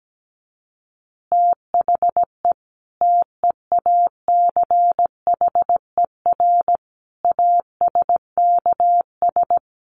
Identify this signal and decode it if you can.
Morse code, heard as a keyed tone, about 17 wpm, 700 Hz. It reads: THE TEACHER ASKS
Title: Additional Easy Phrases 17wpm